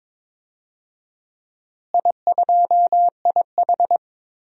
Morse code, heard as a keyed tone, about 22 wpm, 700 Hz. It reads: I2IH
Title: DXCC-100 Call Signs - 4 Characters - Repeated 2x - 22wpm